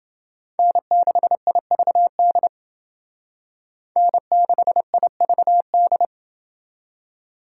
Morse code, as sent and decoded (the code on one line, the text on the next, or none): -. -.... .. ...- -.. / -. -.... .. ...- -..
N6IVD N6IVD